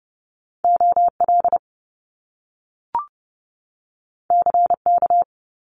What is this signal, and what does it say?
Morse code, heard as a keyed tone, about 30 wpm, 700 Hz. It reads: O L C K
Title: Top Two Letter Combinations in Words 30wpm